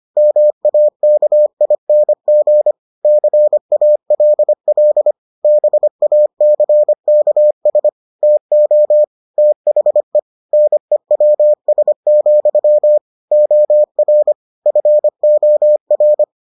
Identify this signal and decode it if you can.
Morse code, heard as a keyed tone, about 25 wpm, 600 Hz. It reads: MAKING CALL BACKS TO THE NEWS, OR FOR